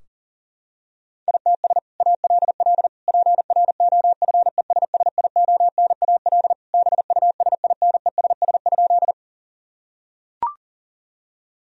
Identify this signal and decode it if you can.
Morse code, heard as a keyed tone, about 40 wpm, 700 Hz. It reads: ITS ALL PROFESSIONAL BUSINESS?
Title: Sentences from Top 1000 Words - Review Part 4 - 40wpm